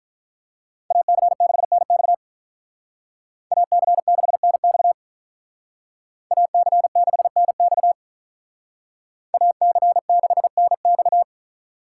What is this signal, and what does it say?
Morse code, heard as a keyed tone, about 44 wpm, 700 Hz. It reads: AC6NX AC6NX AC6NX AC6NX